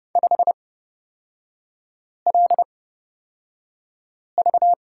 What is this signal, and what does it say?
Morse code, heard as a keyed tone, about 30 wpm, spaced as for 5 wpm, 700 Hz. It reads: 5LV
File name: Lesson 36e - BT - Three Characters 30wpm